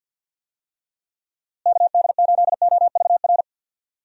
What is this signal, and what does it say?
Morse code, heard as a keyed tone, about 50 wpm, 700 Hz. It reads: KD8OVR
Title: USA Call Signs - 6 Characters - Repeated 2x - 50wpm